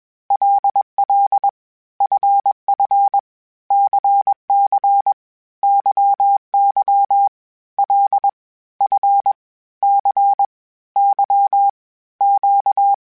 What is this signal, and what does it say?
Morse code, heard as a keyed tone, about 21 wpm, 800 Hz. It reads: LL FF CC YY L F C Y Q
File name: Basic Sets / Basic Letter Groups Collective7 / LFCYQ@20